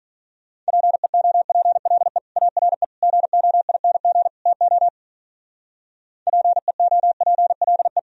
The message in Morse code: PEOPLE ARE GOING TO PEOPLE

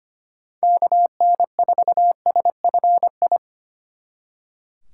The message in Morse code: KN4SFI

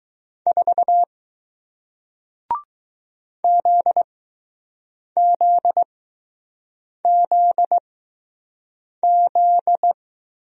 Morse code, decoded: 4 ZZZZ

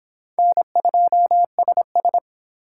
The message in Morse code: N2SS

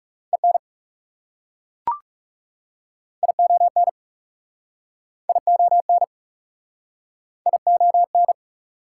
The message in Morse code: EN ION ION ION

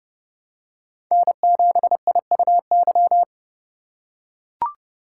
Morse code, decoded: N7IUY

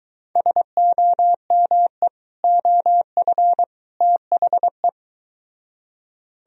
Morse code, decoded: SOME OF THE

